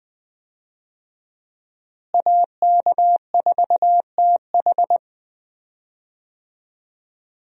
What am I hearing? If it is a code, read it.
Morse code, heard as a keyed tone, about 21 wpm, 700 Hz. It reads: AK4TH
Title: USA Call Signs - 5 Characters - Repeated 3x - 20wpm